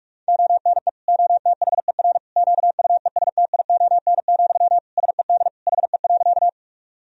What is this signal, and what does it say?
Morse code, heard as a keyed tone, about 46 wpm, 700 Hz. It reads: ONE OTHER QUESTION, SED HE.
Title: Sentences from Top 400 Words and Abbreviations 45wpm